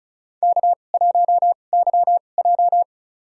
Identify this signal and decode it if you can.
Morse code, heard as a keyed tone, about 34 wpm, spaced as for 25 wpm, 700 Hz. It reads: K1YJ
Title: USA Call Signs - 2x Character Spacing - 35wpm